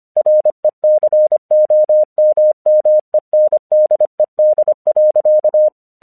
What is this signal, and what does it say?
Morse code, heard as a keyed tone, about 25 wpm, 610 Hz. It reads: RECOMMENDED.
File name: 1 English 25wpm / AQC17